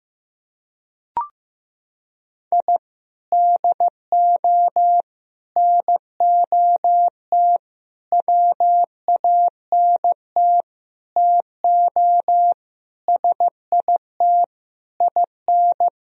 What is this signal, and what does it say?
Morse code, heard as a keyed tone, about 15 wpm, 700 Hz. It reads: I DO NOT WANT TO SIT IN
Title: Sentences from Top 1000 Words - Review Part 2 - 15wpm